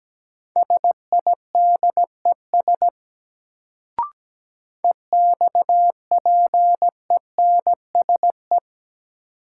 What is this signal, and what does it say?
Morse code, heard as a keyed tone, about 17 wpm, 700 Hz. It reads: SIDES EXPENSE
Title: Sets of 4 Words - New Word Drill - 17wpm